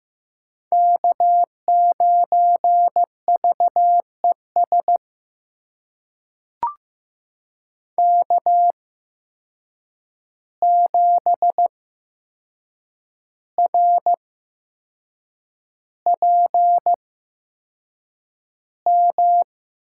Morse code, heard as a keyed tone, about 15 wpm, spaced as for 6 wpm, 700 Hz. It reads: K9VES K7RPM
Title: USA Call Signs - 5 Characters - 8x Character Spacing - 15wpm